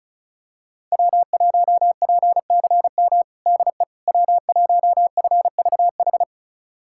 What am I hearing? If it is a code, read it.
Morse code, heard as a keyed tone, about 36 wpm, 700 Hz. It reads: W1PCM DE W1FVH
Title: USA Callsign DE Callsign - 5 Characters - 35wpm